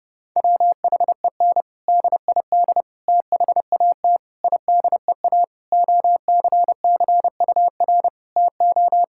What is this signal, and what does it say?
Morse code, heard as a keyed tone, about 32 wpm, 700 Hz. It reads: WHEN DID THAT IDEA OCCUR TO
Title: Sentences from Top 1000 Words - Review Part 6 - 30wpm